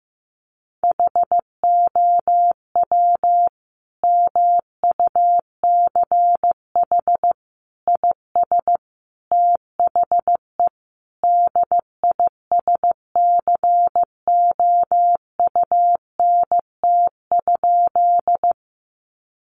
Morse code, heard as a keyed tone, about 15 wpm, 700 Hz. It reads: HOW MUCH IS THE DISCOUNT?
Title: Sentences from Top 1200 Words 15wpm